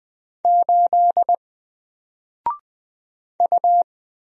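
Morse code, heard as a keyed tone, about 20 wpm, 700 Hz. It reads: 8 U
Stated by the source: Single letter and number Morse drill at 20 wpm, from N4LCD.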